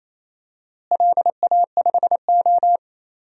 Morse code, heard as a keyed tone, about 28 wpm, 700 Hz. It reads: LA5O
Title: DXCC-100 Call Signs - 4 Characters - Repeated 3x - 28wpm